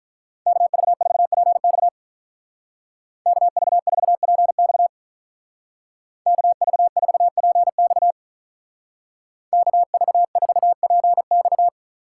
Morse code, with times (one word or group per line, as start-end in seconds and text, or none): 0.5-1.9 KV4PX
3.3-4.9 KV4PX
6.3-8.1 KV4PX
9.5-11.7 KV4PX